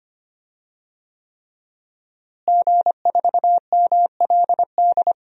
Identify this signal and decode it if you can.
Morse code, heard as a keyed tone, about 25 wpm, 700 Hz. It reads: G4MLD
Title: DXCC-100 Call Signs - 5 Characters - Repeated 3x - 25wpm